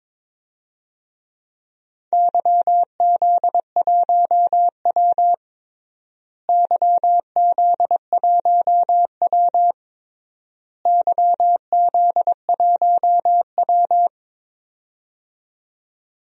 Morse code, heard as a keyed tone, about 22 wpm, 700 Hz. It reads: YZ1W YZ1W YZ1W
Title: DXCC-100 Call Signs - 4 Characters - Repeated 3x - 22wpm